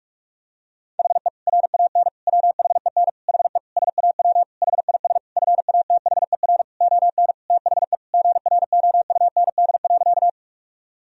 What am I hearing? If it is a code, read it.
Morse code, heard as a keyed tone, about 46 wpm, 700 Hz. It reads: HE RAN WHEN HE SAW HIS FATHER ON THE GROUND.
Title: Sentences from Top 1000 Words - Review Part 3 - 45wpm